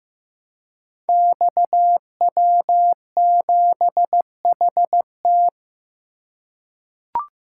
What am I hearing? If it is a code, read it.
Morse code, heard as a keyed tone, about 15 wpm, 700 Hz. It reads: XW7HT
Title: Code Groups - 5 Letter:Numbers - 15wpm